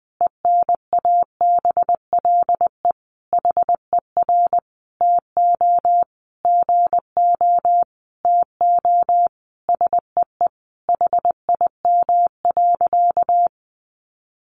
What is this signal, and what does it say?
Morse code, heard as a keyed tone, about 19 wpm, 700 Hz. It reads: ENABLE HER TO GO TO SEE HIM.
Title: Sentences from Top 1500 Words 20wpm